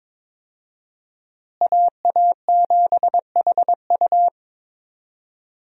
Morse code, heard as a keyed tone, about 22 wpm, 700 Hz. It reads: AA7HU
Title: USA Call Signs - 5 Characters - Repeated 2x - 22wpm